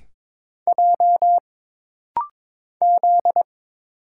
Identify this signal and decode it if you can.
Morse code, heard as a keyed tone, about 22 wpm, 700 Hz. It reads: J Z
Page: This rapid-fire timing is geared to push your ICR (Instant Character Recognition) skills.